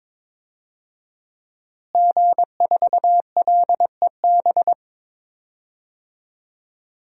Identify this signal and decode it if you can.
Morse code, heard as a keyed tone, about 22 wpm, 700 Hz. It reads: G4LEB
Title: DXCC-100 Call Signs - 5 Characters - Repeated 3x - 22wpm